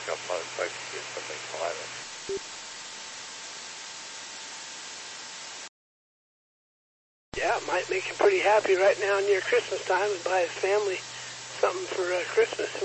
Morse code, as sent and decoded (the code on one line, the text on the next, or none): none